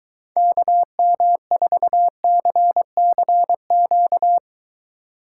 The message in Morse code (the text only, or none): KM4CCQ